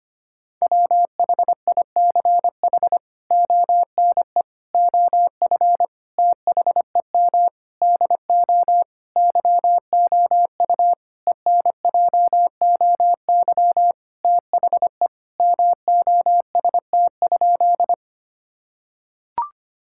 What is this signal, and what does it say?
Morse code, heard as a keyed tone, about 25 wpm, 700 Hz. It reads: WHICH ONE OF THEM DO YOU ENJOY THE MOST?
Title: Sentences from Top 1000 Words - Review Part 5 - 25wpm